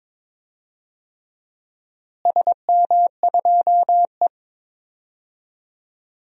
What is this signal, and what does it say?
Morse code, heard as a keyed tone, about 21 wpm, 700 Hz. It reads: SM2E
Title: DXCC-100 Call Signs - 4 Characters - Repeated 3x - 22wpm